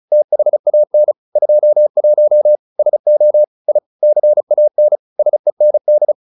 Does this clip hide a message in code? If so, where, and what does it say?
Yes: Morse code from 0.1 to 6.1 s, heard as a keyed tone, about 34 wpm, 600 Hz. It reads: THAN 21 SO I CAN SEND